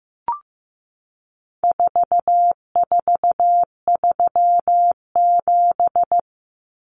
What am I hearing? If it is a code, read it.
Morse code, heard as a keyed tone, about 15 wpm, 700 Hz. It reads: 4437
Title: Numbers - 4-Digit 15wpm